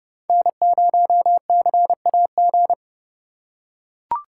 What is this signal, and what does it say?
Morse code, heard as a keyed tone, about 30 wpm, 700 Hz. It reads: N0CAG
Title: USA Call Signs - 3x Character Spacing - 30wpm